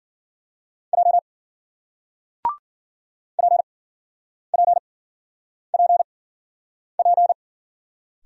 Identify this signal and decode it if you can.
Morse code, heard as a keyed tone, about 53 wpm, spaced as for 9 wpm, 700 Hz. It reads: J PPPP